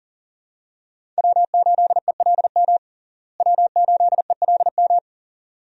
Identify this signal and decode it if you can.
Morse code, heard as a keyed tone, about 40 wpm, 700 Hz. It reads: W8ELM W8ELM
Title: USA Call Signs - 5 Characters - Repeated 2x - 40wpm